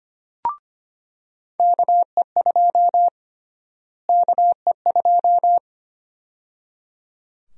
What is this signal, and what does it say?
Morse code, heard as a keyed tone, about 25 wpm, 700 Hz. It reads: KE2 KE2
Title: USA Call Sign Prefixes - Repeated 2x - 25wpm